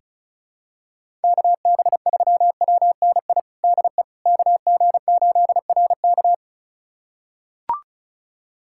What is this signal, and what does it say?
Morse code, heard as a keyed tone, about 36 wpm, 700 Hz. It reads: KB3WNI DE KG8RK
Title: USA Callsign DE Callsign 35wpm